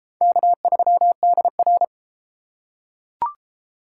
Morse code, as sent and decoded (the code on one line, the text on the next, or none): -.- ...-- -.. .-.
K3DR